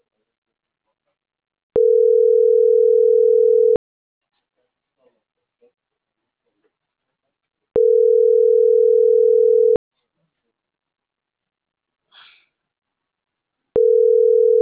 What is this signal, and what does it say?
Ringback tone (North America), 440 + 480 Hz.